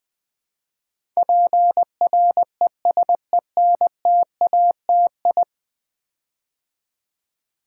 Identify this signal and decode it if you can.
Morse code, heard as a keyed tone, about 20 wpm, 700 Hz. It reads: PRESENTATI
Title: Build A Word - 5 Letters At A Time - 12-Letter Words - 20wpm